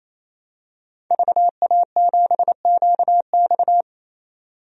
Morse code, heard as a keyed tone, about 27 wpm, 700 Hz. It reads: VA7QX
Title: DXCC-100 Call Signs - 5 Characters - 28wpm